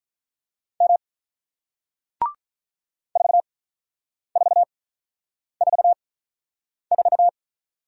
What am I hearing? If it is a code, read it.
Morse code, heard as a keyed tone, about 48 wpm, spaced as for 9 wpm, 700 Hz. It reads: M 4444